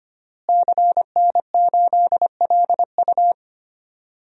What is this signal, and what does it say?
Morse code, heard as a keyed tone, about 25 wpm, 700 Hz. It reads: CN8LU
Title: DXCC-100 Call Signs - 5 Characters - Repeated 2x - 25wpm